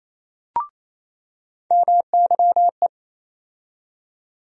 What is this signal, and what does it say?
Morse code, heard as a keyed tone, about 28 wpm, 700 Hz. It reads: MYE
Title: USA Call Sign Postfixes - 3 Letters 28wpm